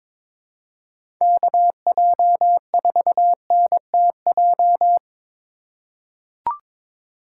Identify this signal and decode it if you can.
Morse code, heard as a keyed tone, about 23 wpm, 700 Hz. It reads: KJ4NTJ
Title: USA Call Signs - 6 Characters - 3x Character Spacing - 22wpm